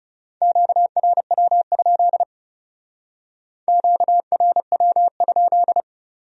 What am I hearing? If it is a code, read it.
Morse code, heard as a keyed tone, about 39 wpm, 700 Hz. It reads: QRW? QRW?